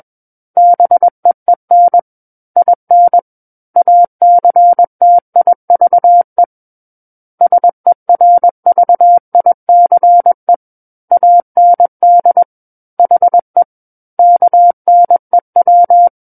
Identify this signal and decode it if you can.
Morse code, heard as a keyed tone, about 21 wpm, 700 Hz. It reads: BEEN IN ACTIVE SERVICE AND HE KNEW
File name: Book One / 12-Chapter 12 - 21 WPM